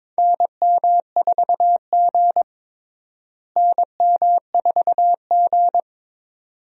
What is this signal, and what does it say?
Morse code, heard as a keyed tone, about 22 wpm, 700 Hz. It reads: NM4G NM4G
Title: USA Call Signs - 4 Characters - Repeated 2x - 22wpm